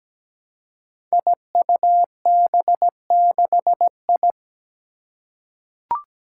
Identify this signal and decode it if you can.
Morse code, heard as a keyed tone, about 17 wpm, 700 Hz. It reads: IUB6I
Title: Five Challenging Letter-Number Combinations 17wpm